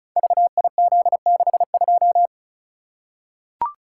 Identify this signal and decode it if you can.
Morse code, heard as a keyed tone, about 36 wpm, 700 Hz. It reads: VIZ62
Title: Code Groups - 5 Letter:Numbers - 35wpm